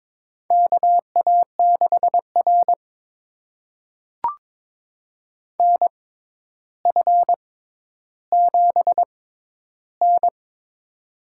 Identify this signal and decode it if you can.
Morse code, heard as a keyed tone, about 22 wpm, spaced as for 11 wpm, 700 Hz. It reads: KA6R NF7N
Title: USA Call Signs - 6x Character Spacing - 22wpm